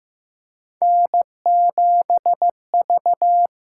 Morse code, heard as a keyed tone, about 15 wpm, 700 Hz. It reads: N7V
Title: USA Call Signs - 5 Characters - 6x Character Spacing - 15wpm